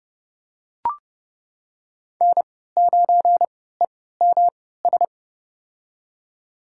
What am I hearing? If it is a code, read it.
Morse code, heard as a keyed tone, about 30 wpm, spaced as for 17 wpm, 700 Hz. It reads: N9EMS
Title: USA Call Signs - 3x Character Spacing - 30wpm